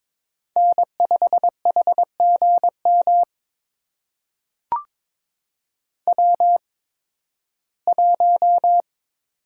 Morse code, heard as a keyed tone, about 22 wpm, spaced as for 14 wpm, 700 Hz. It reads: N5HGM W1
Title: USA Call Signs - 5 Characters - 8x Character Spacing - 22wpm